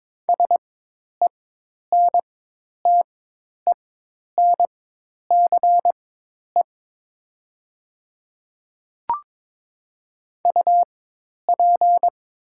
Morse code, heard as a keyed tone, about 22 wpm, spaced as for 10 wpm, 700 Hz. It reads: SENTENCE UP
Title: 4x Character Spacing - Top 200 Words - 22wpm